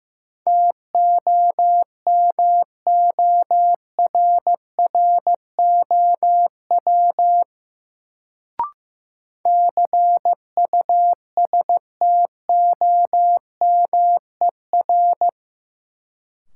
Morse code, heard as a keyed tone, about 15 wpm, 700 Hz. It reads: TOMORROW CUSTOMER
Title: Sentences from Top 1100 Words - New Word Drill Alt - 15wpm